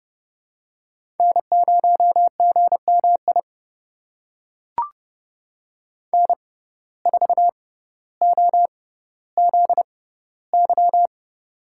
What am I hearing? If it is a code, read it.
Morse code, heard as a keyed tone, about 32 wpm, spaced as for 16 wpm, 700 Hz. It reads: N0GMI N4OZY
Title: USA Call Signs - 5 Characters - 6x Character Spacing - 30wpm